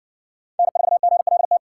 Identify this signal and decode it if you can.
Morse code, heard as a keyed tone, about 60 wpm, 700 Hz. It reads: N4GLT